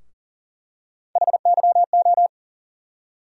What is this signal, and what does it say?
Morse code, heard as a keyed tone, about 40 wpm, 700 Hz. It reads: HYO
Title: USA Call Sign Postfixes - 3 Letters 40wpm